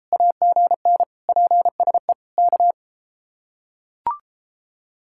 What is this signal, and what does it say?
Morse code, heard as a keyed tone, about 31 wpm, 700 Hz. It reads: AGN PSE K